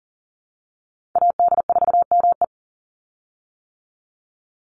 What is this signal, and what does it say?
Morse code, heard as a keyed tone, about 40 wpm, 700 Hz. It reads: AD4ME
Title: USA Call Signs 40wpm